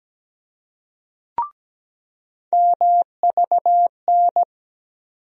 Morse code, heard as a keyed tone, about 17 wpm, 700 Hz. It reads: MVN
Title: USA Call Sign Postfixes - 3 Letters 17wpm